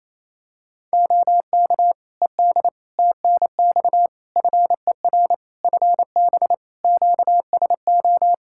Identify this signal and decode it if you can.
Morse code, heard as a keyed tone, about 28 wpm, 700 Hz. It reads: OK ED TNX FER FB QSO
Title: Practice QSO - Exchange 4 - Ending - 28wpm